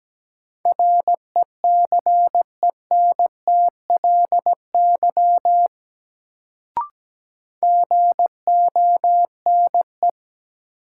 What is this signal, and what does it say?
Morse code, heard as a keyed tone, about 17 wpm, 700 Hz. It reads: RECENTLY GONE
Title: Sentences from Top 1200 Words - New Word Drill - 17wpm